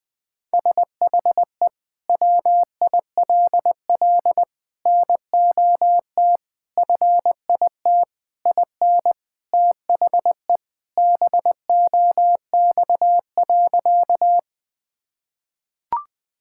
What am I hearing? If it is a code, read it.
Morse code, heard as a keyed tone, about 20 wpm, 700 Hz. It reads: SHE WILL NOT FIT IN THE BOX.
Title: Sentences from Top 1000 Words - Review Part 5 - 20wpm